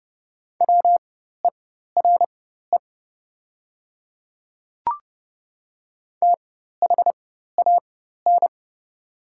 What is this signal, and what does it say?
Morse code, heard as a keyed tone, about 29 wpm, spaced as for 14 wpm, 700 Hz. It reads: WERE THAN
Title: 4x Character Spacing - Top 100 Words - 30wpm